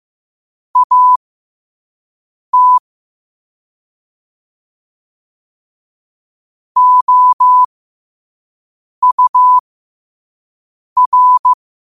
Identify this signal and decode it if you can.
Morse code, heard as a keyed tone, about 15 wpm, spaced as for 5 wpm, 1000 Hz. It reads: AT OUR
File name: qotdfiles / 5wpm